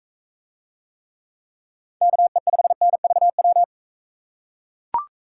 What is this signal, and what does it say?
Morse code, heard as a keyed tone, about 44 wpm, 700 Hz. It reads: KE5NVW